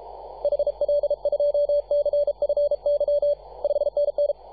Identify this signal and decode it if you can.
Morse code, heard as a keyed tone, about 33 wpm, 580 Hz, with radio noise. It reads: HL2CFY 5NN
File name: posts / 2015-sept-24-2118z-t2gc-18-071-cw